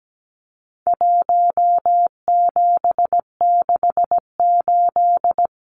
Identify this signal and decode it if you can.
Morse code, heard as a keyed tone, about 17 wpm, 700 Hz. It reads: 1768
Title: Numbers - Contest Style - 4-Digit 17wpm